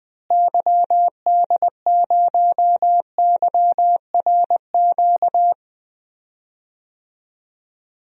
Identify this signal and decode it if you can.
Morse code, heard as a keyed tone, about 21 wpm, 700 Hz. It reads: YD0YRQ
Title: DXCC-100 Call Signs - 6 Characters - Repeated 3x - 20wpm